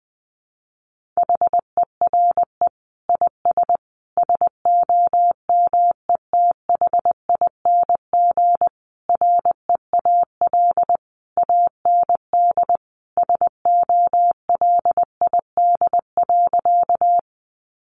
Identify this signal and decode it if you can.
Morse code, heard as a keyed tone, about 20 wpm, 700 Hz. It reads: HERE IS SOMETHING REAL AND SOLID.
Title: Sentences from Top 1500 Words 20wpm